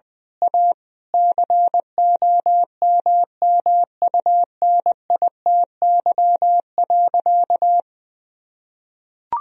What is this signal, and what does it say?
Morse code, heard as a keyed tone, about 21 wpm, 700 Hz. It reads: A COMMUNITY.
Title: Sentences from Top 1000 Words - Review Part 4 - 20wpm